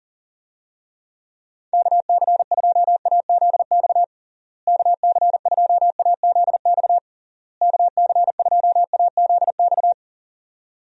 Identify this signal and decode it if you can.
Morse code, heard as a keyed tone, about 40 wpm, 700 Hz. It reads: KC2AZX KC2AZX KC2AZX
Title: USA Call Signs - 6 Characters - Repeated 3x - 40wpm